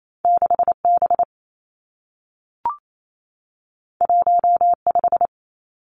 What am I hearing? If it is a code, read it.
Morse code, heard as a keyed tone, about 28 wpm, 700 Hz. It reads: 6 B 1 5
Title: Two Challenging Letter-Number Combinations 28wpm